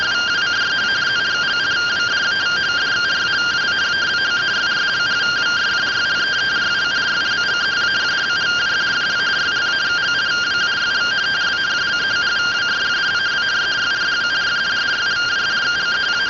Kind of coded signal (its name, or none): two-tone teleprinter signal